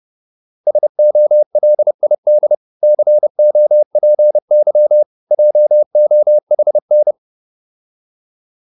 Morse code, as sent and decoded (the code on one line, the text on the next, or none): ... --- .-.. .. -.. / -.-. --- .--. -.-- / .--- --- .... -.
SOLID COPY JOHN